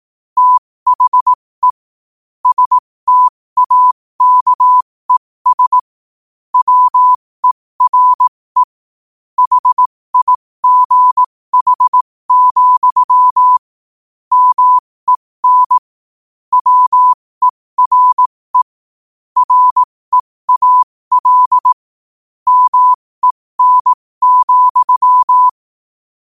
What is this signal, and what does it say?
Morse code, heard as a keyed tone, about 18 wpm, spaced as for 15 wpm, 1000 Hz. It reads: THE STAKES WERE HIGH, MEN WERE REAL MEN,
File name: qotdfiles / 15wpm